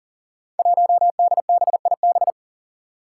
Morse code, heard as a keyed tone, about 40 wpm, 700 Hz. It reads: 1DBIB
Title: Five Challenging Letter-Number Combinations 40wpm